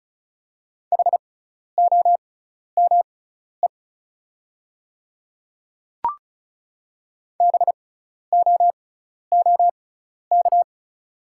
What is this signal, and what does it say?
Morse code, heard as a keyed tone, about 34 wpm, spaced as for 12 wpm, 700 Hz. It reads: HOME BOOK
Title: 6x Character Spacing - Top 300 Words - 35wpm